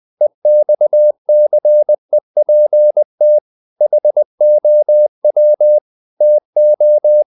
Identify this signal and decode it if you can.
Morse code, heard as a keyed tone, about 20 wpm, 600 Hz. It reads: EXCEPT HOW TO
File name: CW / Practice / EnglishQuotations0320